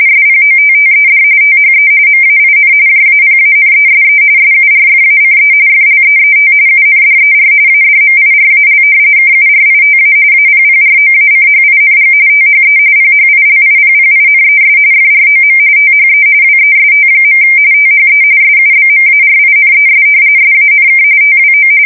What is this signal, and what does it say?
RTTY, about 45.45 baud. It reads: SPOTTED THE MAIN JAPANESE SURFACE FLEET UNDER ADMIRAL KURITA APPROACHING THE PHILIPPINES FROM THE WEST. THE SUBMARINES SANK TWO
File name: fsk170 / battleshipnj